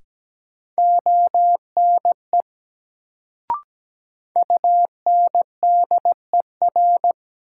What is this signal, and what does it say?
Morse code, heard as a keyed tone, about 17 wpm, 700 Hz. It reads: ONE UNDER
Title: Sets of 2 Words - New Word Drill - 17wpm